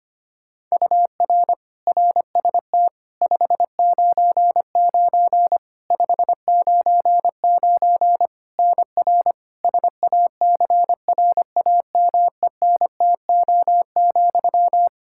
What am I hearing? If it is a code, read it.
Morse code, heard as a keyed tone, about 25 wpm, 700 Hz. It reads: UR RST 599 599 NR SACRAMENTO,
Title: K1USN SST QSO Practice 25wpm